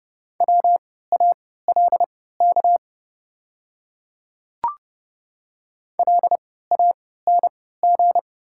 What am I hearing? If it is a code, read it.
Morse code, heard as a keyed tone, about 30 wpm, spaced as for 17 wpm, 690 Hz. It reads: WALK LANG
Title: 3x Character Spacing - Top 500 Words - 30wpm